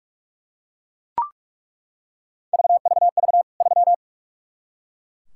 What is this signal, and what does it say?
Morse code, heard as a keyed tone, about 41 wpm, 700 Hz. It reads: VVV 3
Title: K1USN SST QSO Practice 45wpm